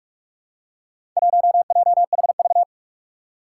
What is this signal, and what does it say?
Morse code, heard as a keyed tone, about 46 wpm, 700 Hz. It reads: 1JHV
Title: Four Challenging Letter-Number Combinations 45wpm